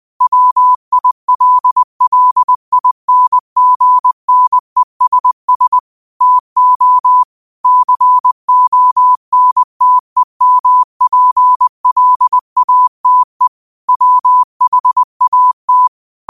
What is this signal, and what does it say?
Morse code, heard as a keyed tone, about 20 wpm, 1000 Hz. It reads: WILLINGNESS TO CONTEMPLATE WHAT
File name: qotdfiles / 20wpm